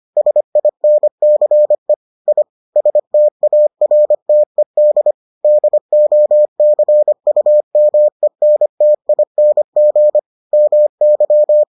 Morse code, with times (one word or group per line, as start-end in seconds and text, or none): 0.2-1.9 SINCE
2.3-2.4 I
2.8-5.1 STARTED
5.4-10.2 DOCUMENTING
10.5-11.6 MY